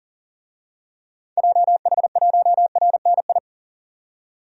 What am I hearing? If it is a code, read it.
Morse code, heard as a keyed tone, about 40 wpm, 700 Hz. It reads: JH1RNI
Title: DXCC-100 Call Signs - 6 Characters - 40wpm